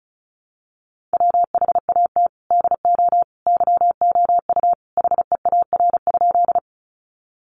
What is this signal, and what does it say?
Morse code, heard as a keyed tone, about 34 wpm, 700 Hz. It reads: WHAT DO YOU HEAR?
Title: Sentences from Top 300 Words 35wpm